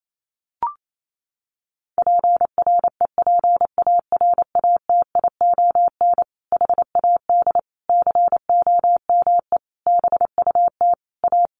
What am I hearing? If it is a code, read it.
Morse code, heard as a keyed tone, about 28 wpm, 700 Hz. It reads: PREPARATION HAD COME BUT A
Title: Sentences from Top 1400 Words 28wpm